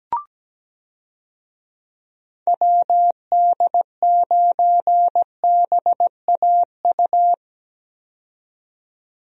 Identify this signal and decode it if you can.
Morse code, heard as a keyed tone, about 17 wpm, 700 Hz. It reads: WD9BAU
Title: USA Call Signs - 6 Characters - 17wpm